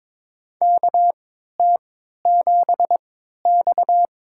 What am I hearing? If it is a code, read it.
Morse code, heard as a keyed tone, about 22 wpm, spaced as for 13 wpm, 700 Hz. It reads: KT7X
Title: USA Call Signs - 4 Characters - 3x Character Spacing - 22wpm